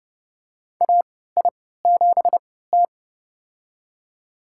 Morse code, heard as a keyed tone, about 32 wpm, spaced as for 17 wpm, 700 Hz. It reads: AI7T
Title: USA Call Signs - 4 Characters - 3x Character Spacing - 30wpm